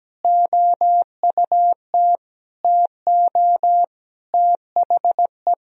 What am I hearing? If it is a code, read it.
Morse code, heard as a keyed tone, about 17 wpm, 700 Hz. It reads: OUT TO THE
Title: Sentences from Top 1000 Words - Review Part 2 - 17wpm